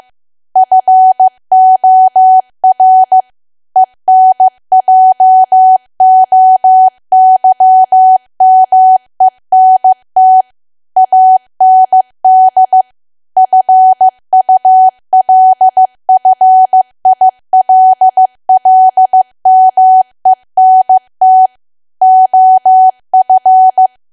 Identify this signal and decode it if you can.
Morse code, heard as a keyed tone, about 15 wpm, 750 Hz. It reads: FOR ENJOYMENT AND FULFILLMENT OF
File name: cwa / W1AW-Level2 / 15WPM